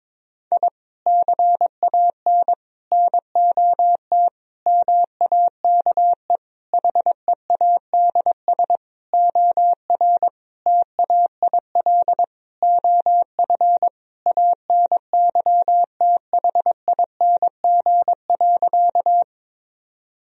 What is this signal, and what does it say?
Morse code, heard as a keyed tone, about 22 wpm, 700 Hz. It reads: I CAN NOT MAKE HEADS OR TAIL OF ANYTHING.
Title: Sentences from Top 1000 Words - Review Part 3 - 22wpm